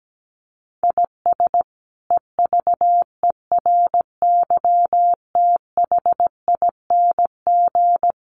Morse code, heard as a keyed tone, about 17 wpm, 700 Hz. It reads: IS EVERYTHING
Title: Sentences from Top 700 Words 17wpm